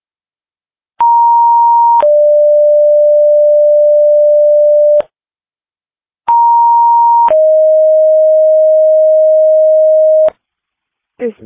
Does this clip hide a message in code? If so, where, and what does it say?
Yes: two-tone page from 1.0 to 10.3 s, heard as a one-second tone, then a three-second tone, about 940 then 590 Hz, about 940 then 620 Hz, with speech.